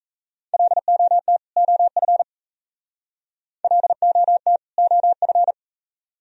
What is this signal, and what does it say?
Morse code, heard as a keyed tone, about 35 wpm, 700 Hz. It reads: LOT OF LOT OF
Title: Speed-Racing - Sets of 2 Words 28wpm